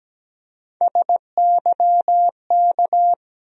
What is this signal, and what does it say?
Morse code, heard as a keyed tone, about 17 wpm, 700 Hz. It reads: SYK
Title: USA Call Sign Postfixes - 3 Letters 17wpm